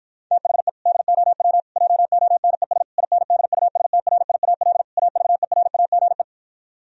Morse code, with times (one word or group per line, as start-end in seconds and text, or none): 0.3-0.7 THE
0.9-1.6 DOW
1.8-2.8 JONES
3.0-4.8 INDUSTRIAL
5.0-6.2 AVERAGE